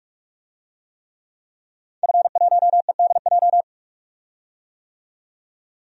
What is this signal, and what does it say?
Morse code, heard as a keyed tone, about 46 wpm, 700 Hz. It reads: F1EDJ
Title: DXCC-100 Call Signs - 5 Characters - 45wpm